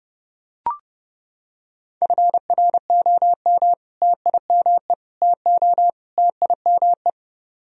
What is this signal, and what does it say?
Morse code, heard as a keyed tone, about 30 wpm, 700 Hz. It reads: FROM TIME TO TIME